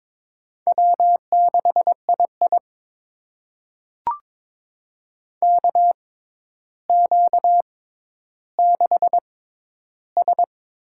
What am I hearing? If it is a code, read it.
Morse code, heard as a keyed tone, about 22 wpm, spaced as for 11 wpm, 700 Hz. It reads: W6II KQ6S
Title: USA Call Signs - 4 Characters - 6x Character Spacing - 22wpm